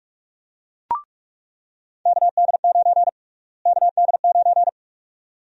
Morse code, heard as a keyed tone, about 46 wpm, 700 Hz. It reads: KD9 KD9
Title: USA Call Sign Prefixes - Repeated 2x - 45wpm